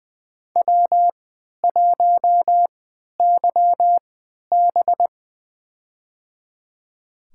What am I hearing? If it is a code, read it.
Morse code, heard as a keyed tone, about 20 wpm, spaced as for 11 wpm, 700 Hz. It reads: W1YB
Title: USA Call Signs - 3x Character Spacing - 20wpm